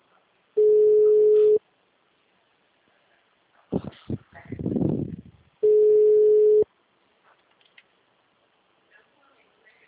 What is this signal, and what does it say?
Ringback tone (Europe, 425 Hz).
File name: out- -2011- - - .690